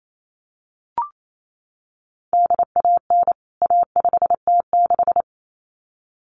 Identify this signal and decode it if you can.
Morse code, heard as a keyed tone, about 28 wpm, 700 Hz. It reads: DAN A5T6
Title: CWT Exchange 28wpm